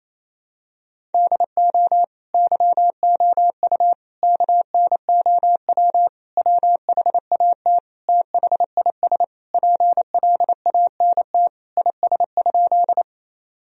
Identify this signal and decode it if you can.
Morse code, heard as a keyed tone, about 28 wpm, 700 Hz. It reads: DO YOU KNOW WHAT THIS PLANT IS?
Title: Sentences from Top 1000 Words - Review Part 1 - 28wpm